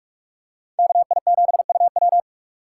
Morse code, heard as a keyed tone, about 42 wpm, 700 Hz. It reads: KI7UW
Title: USA Call Signs - 5 Characters - 6x Character Spacing - 45wpm